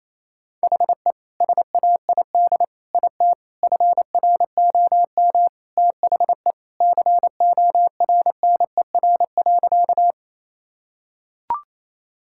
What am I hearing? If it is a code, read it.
Morse code, heard as a keyed tone, about 28 wpm, 700 Hz. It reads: HE SAID IT FROM THE CORNER.
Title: Sentences from Top 1600 Words 28wpm